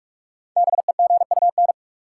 Morse code, heard as a keyed tone, about 41 wpm, 700 Hz. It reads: BEGUN
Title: Sentences from Top 1000 Words - Review Part 6 - New Word Drill - 45wpm